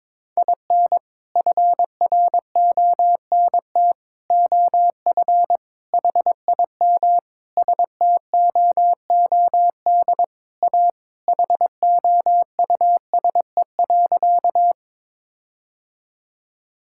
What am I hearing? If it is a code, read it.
Morse code, heard as a keyed tone, about 22 wpm, 700 Hz. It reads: IN FRONT OF HIM STOOD A HOUSE.
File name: practice / 22wpm / Sentences with 4-Word Phrases - 22wpm